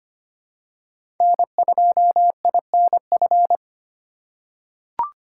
Morse code, heard as a keyed tone, about 25 wpm, 700 Hz. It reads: N2INF
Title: USA Call Signs - 5 Characters - 3x Character Spacing - 25wpm